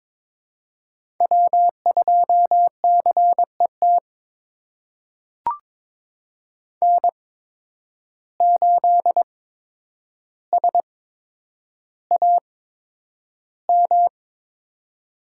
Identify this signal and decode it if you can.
Morse code, heard as a keyed tone, about 23 wpm, spaced as for 10 wpm, 700 Hz. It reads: W2CET N8SAM
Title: USA Call Signs - 5 Characters - 8x Character Spacing - 22wpm